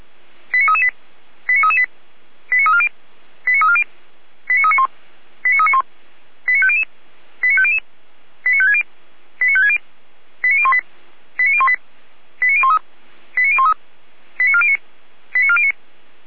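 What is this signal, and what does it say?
Five-tone selective call (ZVEI), heard as quick runs of five tones, tones of about 70 ms. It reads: 89338 89338 89340 89340 89391 89391 89500 89500 89560 89560 89018 89018 89013 89013 89409 89409
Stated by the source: ZVEI five-tone call received by SDR, fire brigade Orlamünde (feuerwehr-orlamuende).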